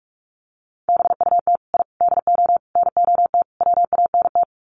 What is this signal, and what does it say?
Morse code, heard as a keyed tone, about 44 wpm, 700 Hz. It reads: BUT I DO NOT WANT
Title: Sets of 5 Words 45wpm